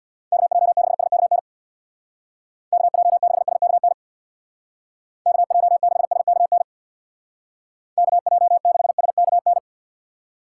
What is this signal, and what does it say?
Morse code, heard as a keyed tone, about 63 wpm, 700 Hz. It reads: KJ6SKN KJ6SKN KJ6SKN KJ6SKN